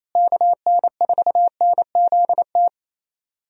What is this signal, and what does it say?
Morse code, heard as a keyed tone, about 27 wpm, 700 Hz. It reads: KN4NZT
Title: USA Call Signs - 6 Characters - Repeated 3x - 28wpm